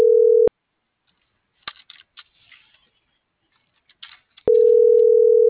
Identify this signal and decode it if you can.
Ringback tone (North America), 440 + 480 Hz.